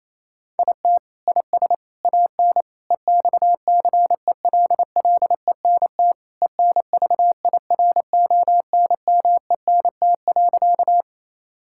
Morse code, heard as a keyed tone, about 28 wpm, 700 Hz. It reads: IT IS AN EXCELLENT ENVIRONMENT.
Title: Sentences from Top 1000 Words - Review Part 5 - 28wpm